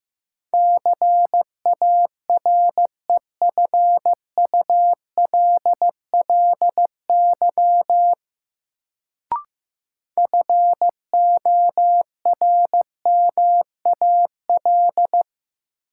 Morse code, heard as a keyed tone, about 15 wpm, 700 Hz. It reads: CAREFULLY FORMAL
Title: Sentences from Top 1300 Words - New Word Drill - 15wpm